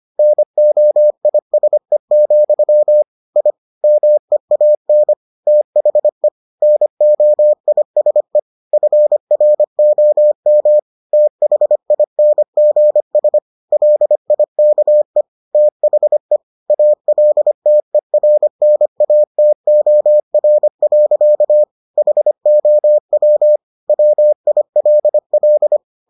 Morse code, heard as a keyed tone, about 25 wpm, 600 Hz. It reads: NOISE, I MEAN THE NOISE FROM THINGS LIKE THE ALTERNATOR. HOW WILL